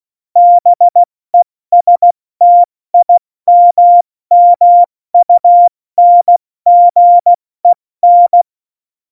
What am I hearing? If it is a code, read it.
Morse code, heard as a keyed tone, about 16 wpm, 700 Hz. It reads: BESTIMMUNGEN
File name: Morsekurs / Witze / Witz 04. Dezember, 70 F80